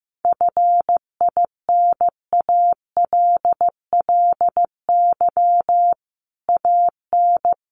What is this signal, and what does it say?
Morse code, heard as a keyed tone, about 15 wpm, 700 Hz. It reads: FINALLY AN
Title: Sentences from Top 1000 Words 15wpm